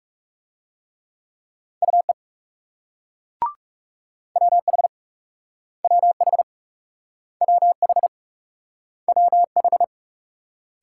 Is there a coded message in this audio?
Yes: Morse code from 1.8 to 9.8 s, heard as a keyed tone, about 39 wpm, 700 Hz. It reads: UE WH WH WH WH